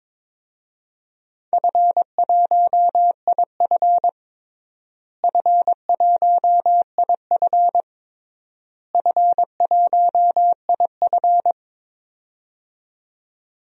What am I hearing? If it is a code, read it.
Morse code, heard as a keyed tone, about 22 wpm, 700 Hz. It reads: F1IF F1IF F1IF
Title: DXCC-100 Call Signs - 4 Characters - Repeated 3x - 22wpm